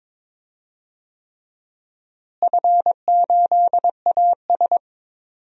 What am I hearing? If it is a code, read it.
Morse code, heard as a keyed tone, about 22 wpm, 700 Hz. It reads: F8AS
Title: DXCC-100 Call Signs - 4 Characters - Repeated 3x - 22wpm